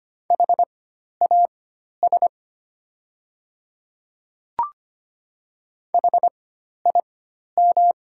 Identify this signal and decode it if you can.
Morse code, heard as a keyed tone, about 25 wpm, spaced as for 12 wpm, 700 Hz. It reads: HAS HIM
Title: 4x Character Spacing - Top 100 Words - 25wpm